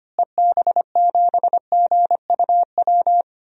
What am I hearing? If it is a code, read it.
Morse code, heard as a keyed tone, about 25 wpm, 700 Hz. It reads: EB7GUW
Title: DXCC-100 Call Signs - 6 Characters - Repeated 2x - 25wpm